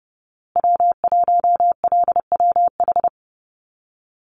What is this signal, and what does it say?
Morse code, heard as a keyed tone, about 30 wpm, 700 Hz. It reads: W1LWH
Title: Call Signs Repeated 3x - 30wpm